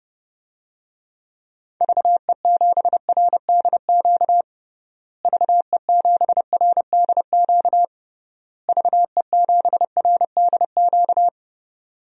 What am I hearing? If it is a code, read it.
Morse code, heard as a keyed tone, about 32 wpm, 700 Hz. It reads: VE7RDQ VE7RDQ VE7RDQ
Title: DXCC-100 Call Signs - 6 Characters - Repeated 3x - 30wpm